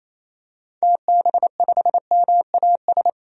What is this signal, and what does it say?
Morse code, heard as a keyed tone, about 27 wpm, 700 Hz. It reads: TB5MAS
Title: DXCC-100 Call Signs - 6 Characters - 28wpm